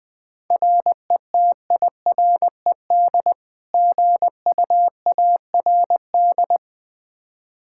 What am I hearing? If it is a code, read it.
Morse code, heard as a keyed tone, about 20 wpm, 700 Hz. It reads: RETIRED GUARD
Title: K1USN SST QSO Practice 20wpm